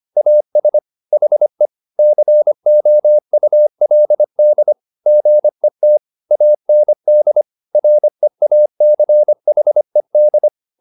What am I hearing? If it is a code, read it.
Morse code, heard as a keyed tone, about 25 wpm, 600 Hz. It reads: AS HE COULD GET AND REACHED
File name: the old man and the sea part 1 25wpm